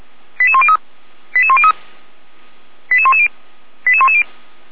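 Five-tone selective call (ZVEI), heard as quick runs of five tones, tones of about 70 ms. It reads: 88103 88103 88110 88110
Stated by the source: ZVEI five-tone call 88104 received with an SDR, recorded on the Feuerwehr Orlamünde server.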